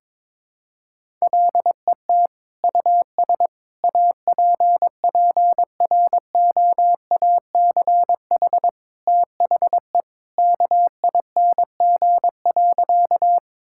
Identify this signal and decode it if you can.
Morse code, heard as a keyed tone, about 22 wpm, 700 Hz. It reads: LET US APPROACH THE KING.
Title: Sentences from Top 1000 Words - Review Part 6 - 22wpm